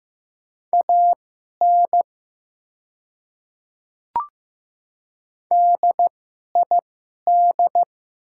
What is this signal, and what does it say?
Morse code, heard as a keyed tone, about 15 wpm, spaced as for 11 wpm, 690 Hz. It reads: AN DID
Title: 2x Character Spacing - Top 100 Words - 15wpm